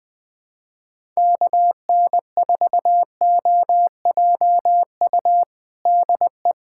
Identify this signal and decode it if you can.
Morse code, heard as a keyed tone, about 20 wpm, 700 Hz. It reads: KN4OJU DE
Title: USA Callsign DE Callsign 20wpm